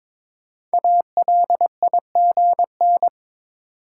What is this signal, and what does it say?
Morse code, heard as a keyed tone, about 22 wpm, 700 Hz. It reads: ALIGN
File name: 5 Letter Words - First 3 Letters Given - 22wpm